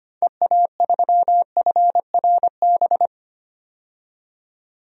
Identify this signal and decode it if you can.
Morse code, heard as a keyed tone, about 25 wpm, 700 Hz. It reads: EA3FRB
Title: DXCC-100 Call Signs - 6 Characters - 25wpm